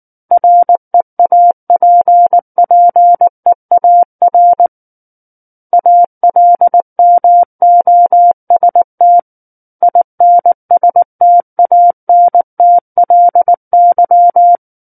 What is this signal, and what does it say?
Morse code, heard as a keyed tone, about 19 wpm, 700 Hz. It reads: REAPPEAR ALMOST INSTANTLY
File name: Book One / 10-Chapter 10 - 19 WPM